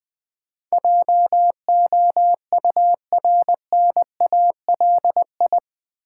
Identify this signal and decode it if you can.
Morse code, heard as a keyed tone, about 20 wpm, 700 Hz. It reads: JOURNALI
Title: Build A Word - 4 Letters At A Time - 10-Letter Words - 20wpm